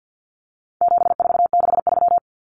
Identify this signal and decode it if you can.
Morse code, heard as a keyed tone, about 48 wpm, 700 Hz. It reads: 7463
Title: Common Cut Numbers - 4-Digit 50wpm